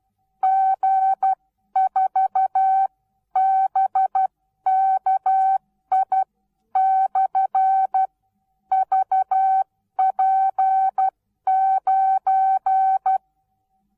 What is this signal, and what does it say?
Morse code, heard as a keyed tone, about 12 wpm, spaced as for 10 wpm, 770 Hz. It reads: G4BKI/VP9
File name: Onlinelehrgang / morsen / DJ4UF Morsen19